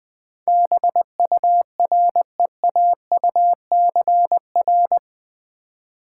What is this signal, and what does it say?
Morse code, heard as a keyed tone, about 20 wpm, 700 Hz. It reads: BUREAUCR
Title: Build A Word - 4 Letters At A Time - 11-Letter Words - 20wpm